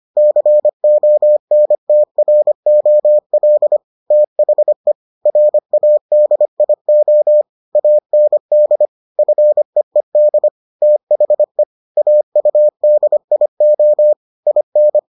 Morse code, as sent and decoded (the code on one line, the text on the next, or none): -.-. --- -. - .-. --- .-.. / - .... / .-. .- -.. .. --- / .- -. -.. / ..-. . -.. / - .... / .- ..- -.. .. --- / .. -.
CONTROL THE RADIO AND FEED THE AUDIO IN